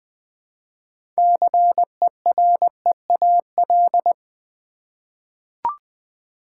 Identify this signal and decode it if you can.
Morse code, heard as a keyed tone, about 20 wpm, 700 Hz. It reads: CEREAL
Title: Easy Words with Hints - 20wpm